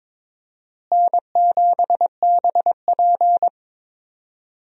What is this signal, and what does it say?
Morse code, heard as a keyed tone, about 22 wpm, 700 Hz. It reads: N7BP
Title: USA Call Signs - 4 Characters - Repeated 2x - 22wpm